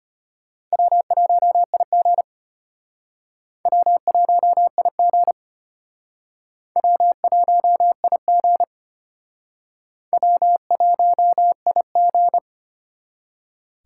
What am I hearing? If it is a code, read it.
Morse code, heard as a keyed tone, about 31 wpm, 700 Hz. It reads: W1IG W1IG W1IG W1IG
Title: Speed Racing - USA 1x2 Call Signs 25wpm